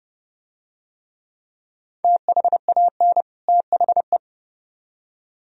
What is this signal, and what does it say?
Morse code, heard as a keyed tone, about 30 wpm, 700 Hz. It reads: THAN THE